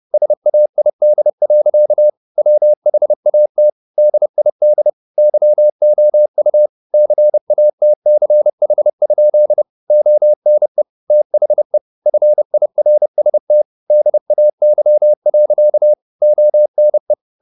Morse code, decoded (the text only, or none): SAID. WHAT DID YOU CATCH? ONE THE FIRST DAY. ONE